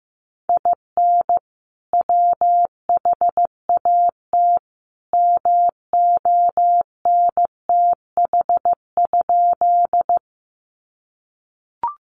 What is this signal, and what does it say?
Morse code, heard as a keyed tone, about 15 wpm, 700 Hz. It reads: IN WHAT MONTH?
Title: Sentences from Top 800 Words 15wpm